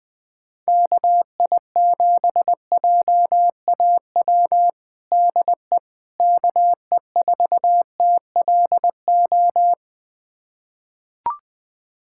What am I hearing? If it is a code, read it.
Morse code, heard as a keyed tone, about 20 wpm, 700 Hz. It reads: KI7JAW DE KE4TLO
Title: USA Callsign DE Callsign 20wpm